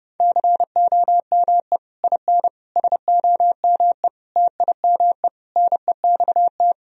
Morse code, heard as a keyed tone, about 30 wpm, 700 Hz. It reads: COME IN SOME TIME NEXT
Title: Sentences from Top 1000 Words - Review Part 2 - 30wpm